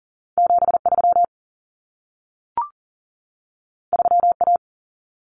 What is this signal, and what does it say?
Morse code, heard as a keyed tone, about 40 wpm, 700 Hz. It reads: 73 3A
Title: Common Cut Numbers - 2-Digit 40wpm